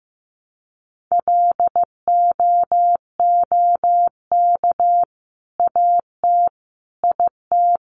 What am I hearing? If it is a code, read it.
Morse code, heard as a keyed tone, about 15 wpm, 700 Hz. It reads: LOOK AT IT
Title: Sentences from Top 100 Words 15wpm